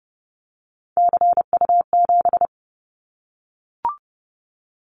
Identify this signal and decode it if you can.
Morse code, heard as a keyed tone, about 30 wpm, 700 Hz. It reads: CU7
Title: Callsign Prefixes for Top 100 DXCC - 30wpm